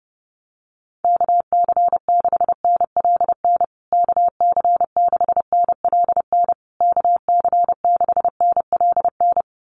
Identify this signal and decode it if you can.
Morse code, heard as a keyed tone, about 30 wpm, 700 Hz. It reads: KC6NLN KC6NLN KC6NLN
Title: USA Call Signs Repeated 3x 30wpm